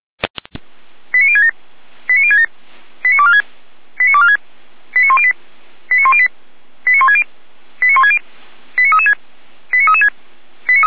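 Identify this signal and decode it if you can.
Five-tone selective call (ZVEI), heard as quick runs of five tones, tones of about 70 ms. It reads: 89067 89067 89246 89246 89108 89108 89160 89160 89336 89336
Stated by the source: ZVEI five-tone call received by SDR, audio.f85360.feuerwehr-orlamuende.org (fire brigade Orlamünde).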